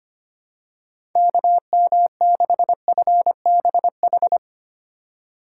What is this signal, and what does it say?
Morse code, heard as a keyed tone, about 25 wpm, 700 Hz. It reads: KM6FBH
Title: USA Call Signs - 6 Characters - 8x Character Spacing - 25wpm